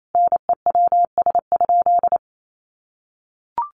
Morse code, heard as a keyed tone, about 28 wpm, 700 Hz. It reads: NEWS?
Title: Sentences from Top 800 Words 28wpm